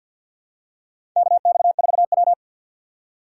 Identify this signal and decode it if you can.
Morse code, heard as a keyed tone, about 48 wpm, 700 Hz. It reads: KX4W
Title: USA Call Signs - 4 Characters - 6x Character Spacing - 50wpm